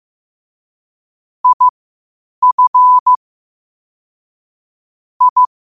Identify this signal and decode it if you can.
Morse code, heard as a keyed tone, about 15 wpm, spaced as for 8 wpm, 1000 Hz. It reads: IF I
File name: qotdfiles / 8wpm